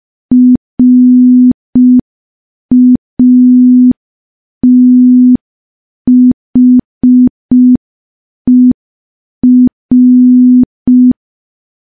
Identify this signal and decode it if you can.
Morse code, heard as a keyed tone, about 5 wpm, 250 Hz. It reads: RATHER
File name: morse / morse-training-05WPM